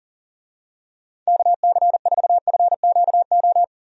Morse code, heard as a keyed tone, about 40 wpm, 700 Hz. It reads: KC4FQO
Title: USA Call Signs - 6 Characters - 40wpm